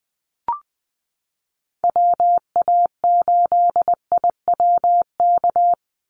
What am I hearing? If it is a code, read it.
Morse code, heard as a keyed tone, about 20 wpm, 700 Hz. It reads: WA8IWK
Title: Call Signs Repeated 2x - 20wpm